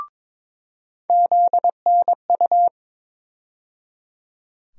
Morse code, heard as a keyed tone, about 23 wpm, 700 Hz. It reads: ZNU
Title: USA Call Sign Postfixes - 3 Letters 22wpm